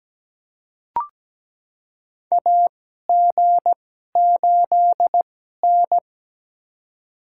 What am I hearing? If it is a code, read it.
Morse code, heard as a keyed tone, about 17 wpm, spaced as for 12 wpm, 700 Hz. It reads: AG8N
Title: USA Call Signs - 2x Character Spacing - 17wpm